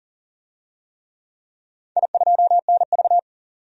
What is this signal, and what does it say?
Morse code, heard as a keyed tone, about 40 wpm, 700 Hz. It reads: I2NV
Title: DXCC-100 Call Signs - 4 Characters - 40wpm